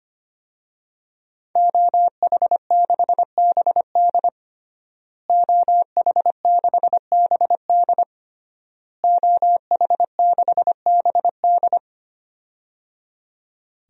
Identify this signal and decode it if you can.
Morse code, heard as a keyed tone, about 25 wpm, 700 Hz. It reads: OH6BD OH6BD OH6BD
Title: DXCC-100 Call Signs - 5 Characters - Repeated 3x - 25wpm